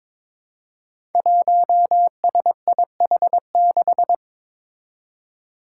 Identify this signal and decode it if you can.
Morse code, heard as a keyed tone, about 21 wpm, 700 Hz. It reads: 1SIH6
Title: Five Challenging Letter-Number Combinations 22wpm